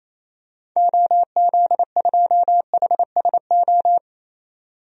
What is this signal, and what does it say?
Morse code, heard as a keyed tone, about 28 wpm, 700 Hz. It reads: OZ2HSO
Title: DXCC-100 Call Signs - 6 Characters - 28wpm